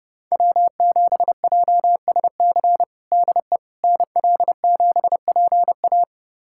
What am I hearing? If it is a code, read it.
Morse code, heard as a keyed tone, about 30 wpm, 700 Hz. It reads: W7JSC DE NL7PA
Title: USA Callsign DE Callsign - 5 Characters - 30wpm